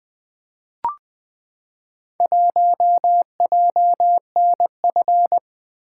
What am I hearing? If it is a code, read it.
Morse code, heard as a keyed tone, about 20 wpm, 700 Hz. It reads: 1JNF
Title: Four Letter-Number Combinations - 20wpm